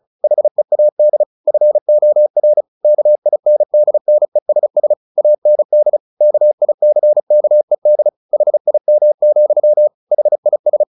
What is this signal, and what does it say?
Morse code, heard as a keyed tone, about 34 wpm, 600 Hz. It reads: HEAD FOR KINDNESS AND KICKED HIM, HIS
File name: the old man and the sea / the old man and the sea 1 35wpm